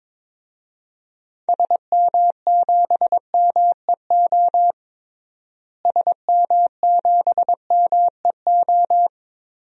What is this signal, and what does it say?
Morse code, heard as a keyed tone, about 22 wpm, 700 Hz. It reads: SM7MEO SM7MEO
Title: DXCC-100 Call Signs - 6 Characters - Repeated 3x - 22wpm